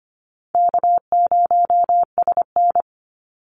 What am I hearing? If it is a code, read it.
Morse code, heard as a keyed tone, about 25 wpm, 700 Hz. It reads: K0SN
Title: Easy Call Signs Repeated 2x - 25wpm